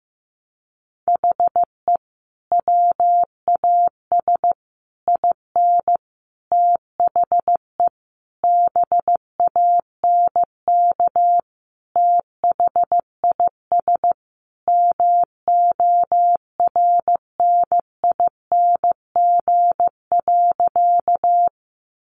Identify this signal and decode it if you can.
Morse code, heard as a keyed tone, about 15 wpm, 700 Hz. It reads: HE WAS IN THE BANK THIS MORNING.
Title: Sentences from Top 900 Words 15wpm